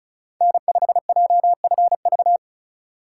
Morse code, heard as a keyed tone, about 34 wpm, 700 Hz. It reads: N5JFV
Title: USA Call Signs - 4x Character Spacing - 35wpm